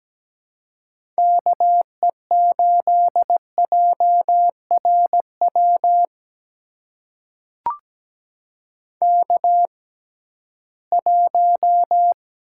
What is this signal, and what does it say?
Morse code, heard as a keyed tone, about 17 wpm, spaced as for 13 wpm, 700 Hz. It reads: KE8JRW K1
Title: USA Call Signs - 6x Character Spacing - 17wpm